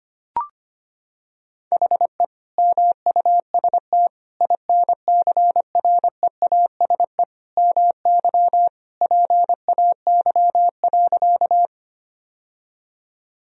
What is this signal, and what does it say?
Morse code, heard as a keyed tone, about 25 wpm, 700 Hz. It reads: HE MUST INCREASE MY PAY.
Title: Sentences from Top 1000 Words - Review Part 4 - 25wpm